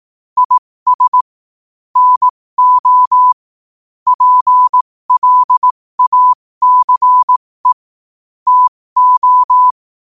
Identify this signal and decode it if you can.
Morse code, heard as a keyed tone, about 18 wpm, spaced as for 15 wpm, 1000 Hz. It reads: IS NO PLACE TO
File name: qotdfiles / 15wpm